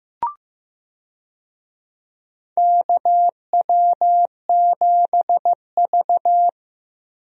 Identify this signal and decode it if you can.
Morse code, heard as a keyed tone, about 15 wpm, 700 Hz. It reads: KW7V
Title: USA Call Signs - 4 Characters - Repeated 2x - 15wpm